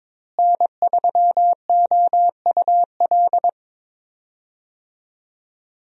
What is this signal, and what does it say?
Morse code, heard as a keyed tone, about 22 wpm, 700 Hz. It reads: N3OUL
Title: USA Call Signs - 5 Characters - Repeated 3x - 22wpm